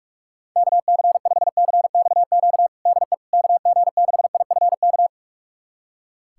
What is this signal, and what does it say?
Morse code, heard as a keyed tone, about 41 wpm, 700 Hz. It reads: KC5CXQ DE KG6IFK
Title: USA Callsign DE Callsign 45wpm